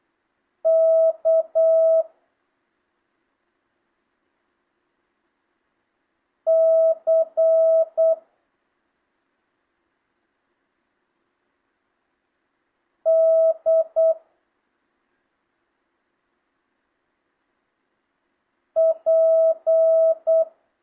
Morse code, heard as a keyed tone, about 8 wpm, 650 Hz. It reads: K C D P
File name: jh8cbh / 2025cw